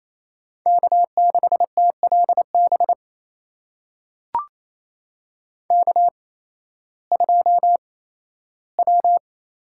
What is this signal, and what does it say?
Morse code, heard as a keyed tone, about 28 wpm, spaced as for 15 wpm, 700 Hz. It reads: K6TLB K2W
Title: USA Call Signs - 8x Character Spacing - 28wpm